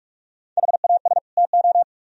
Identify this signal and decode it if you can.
Morse code, heard as a keyed tone, about 55 wpm, 700 Hz. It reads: HAS TO
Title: Speed-Racing - Sets of 2 Words 45wpm